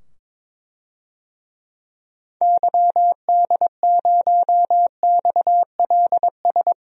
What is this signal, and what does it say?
Morse code, heard as a keyed tone, about 22 wpm, 700 Hz. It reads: YD0XLS
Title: DXCC-100 Call Signs - 6 Characters - 22wpm